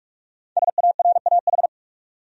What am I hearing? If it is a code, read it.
Morse code, heard as a keyed tone, about 46 wpm, 700 Hz. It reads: SARAH
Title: USA Common Names - Max 5 Letters - 45wpm